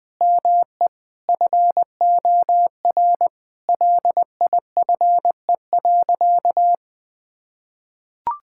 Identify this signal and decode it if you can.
Morse code, heard as a keyed tone, about 20 wpm, 700 Hz. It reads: ME FOR LIFE.
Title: Sentences from Top 1000 Words - Review Part 6 - 20wpm